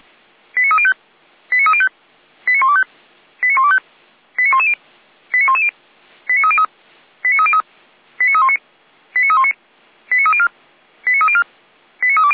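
Five-tone selective call (ZVEI), heard as quick runs of five tones, tones of about 70 ms. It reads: 89305 89305 89125 89125 89110 89110 89303 89303 89319 89319 89304 89304 89320
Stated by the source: ZVEI five-tone selective call received by SDR, Feuerwehr Orlamünde fire brigade.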